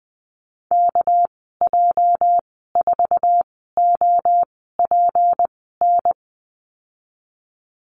Morse code, heard as a keyed tone, about 20 wpm, spaced as for 14 wpm, 700 Hz. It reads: KJ4OPN
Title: 207 Lesson 33g - Letter j - Call Signs 20wpm